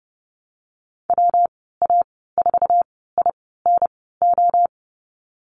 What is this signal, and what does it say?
Morse code, heard as a keyed tone, about 30 wpm, spaced as for 17 wpm, 700 Hz. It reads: WA4INO
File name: Lesson 19g - Letter w - Call Signs 30wpm